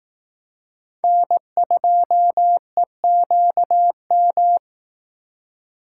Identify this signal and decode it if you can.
Morse code, heard as a keyed tone, about 18 wpm, 700 Hz. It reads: N2EQM
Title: Speed Racing - USA 1x3 Call Signs 15wpm